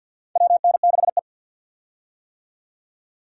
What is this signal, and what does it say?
Morse code, heard as a keyed tone, about 47 wpm, 700 Hz. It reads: WN6E
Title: USA Call Signs - 4 Characters - 50wpm